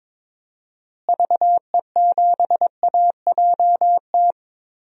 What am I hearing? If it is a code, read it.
Morse code, heard as a keyed tone, about 21 wpm, 700 Hz. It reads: VE7AJT
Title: DXCC-100 Call Signs - 6 Characters - Repeated 2x - 22wpm